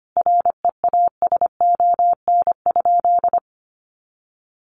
Morse code, heard as a keyed tone, about 25 wpm, 700 Hz. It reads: REASON?
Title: Sentences from Top 700 Words 25wpm